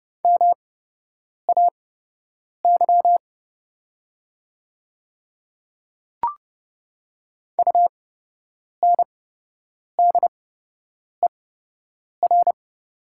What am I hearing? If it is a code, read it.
Morse code, heard as a keyed tone, about 30 wpm, spaced as for 8 wpm, 700 Hz. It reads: MAY UNDER
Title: 8x Character Spacing - Top 500 Words - 30wpm